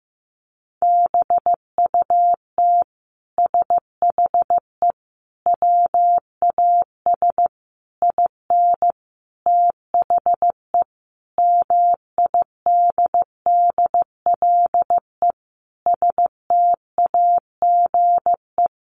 Morse code, heard as a keyed tone, about 15 wpm, 700 Hz. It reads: BUT SHE WAS IN THE MIDDLE STAGE
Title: Sentences from Top 1200 Words 15wpm